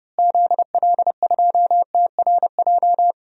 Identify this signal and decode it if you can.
Morse code, heard as a keyed tone, about 32 wpm, 700 Hz. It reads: ZL2TRJ
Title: DXCC-100 Call Signs - 6 Characters - Repeated 2x - 30wpm